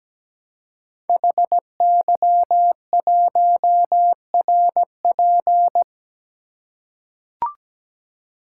Morse code, heard as a keyed tone, about 17 wpm, 700 Hz. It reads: HY1RP
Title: Code Groups - 5 Letter:Numbers - 17wpm